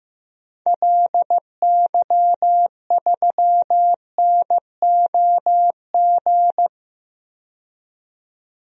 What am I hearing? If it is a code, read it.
Morse code, heard as a keyed tone, about 15 wpm, 700 Hz. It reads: LY3NOG
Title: DXCC-100 Call Signs - 6 Characters - 15wpm